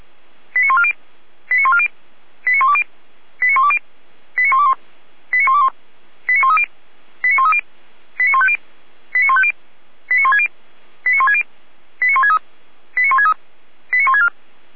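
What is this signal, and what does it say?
Five-tone selective call (ZVEI), heard as quick runs of five tones, tones of about 70 ms. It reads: 89140 89140 89120 89120 89121 89121 89130 89130 89150 89150 89160 89160 89163 89163 89164
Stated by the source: ZVEI five-tone selective call 89119, received with an SDR.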